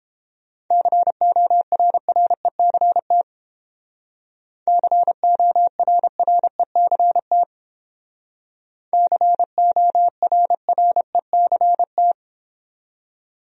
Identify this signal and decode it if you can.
Morse code, heard as a keyed tone, about 31 wpm, 700 Hz. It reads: CORRECT CORRECT CORRECT